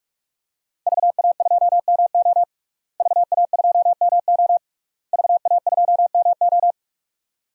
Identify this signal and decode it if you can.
Morse code, heard as a keyed tone, about 41 wpm, 700 Hz. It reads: VA2MO VA2MO VA2MO
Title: DXCC-100 Call Signs - 5 Characters - Repeated 3x - 45wpm